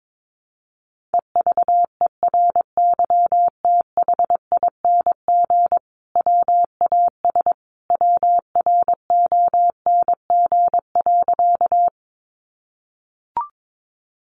Morse code, heard as a keyed tone, about 22 wpm, 700 Hz. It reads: EVERYTHING WAS WRONG.
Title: Sentences from Top 900 Words 22wpm